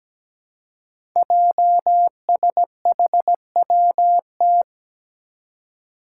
Morse code, heard as a keyed tone, about 17 wpm, 700 Hz. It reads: JSHWT
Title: Five Challenging Letter-Number Combinations 17wpm